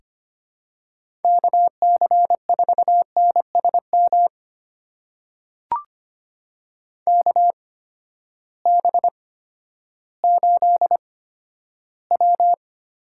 Morse code, heard as a keyed tone, about 25 wpm, spaced as for 12 wpm, 700 Hz. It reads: KC4NSM KB8W
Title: USA Call Signs - 6 Characters - 8x Character Spacing - 25wpm